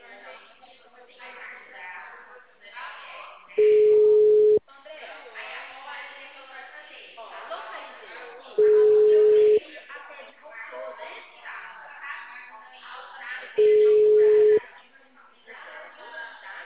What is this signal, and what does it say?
Ringback tone (Europe, 425 Hz), with speech.